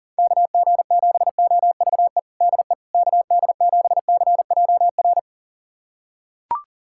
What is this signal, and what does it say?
Morse code, heard as a keyed tone, about 40 wpm, 700 Hz. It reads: KG7OVE DE KD7CJR
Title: USA Callsign DE Callsign 40wpm